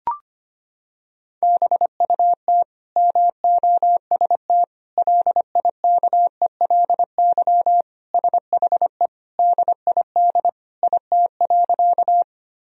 Morse code, heard as a keyed tone, about 25 wpm, 700 Hz. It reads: BUT MOST LIKELY SHE DID IT.
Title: Sentences from Top 1000 Words - Review Part 4 - 25wpm